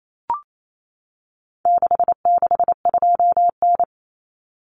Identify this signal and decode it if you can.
Morse code, heard as a keyed tone, about 28 wpm, 700 Hz. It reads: 662N
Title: Common Cut Numbers - 4-Digit 28wpm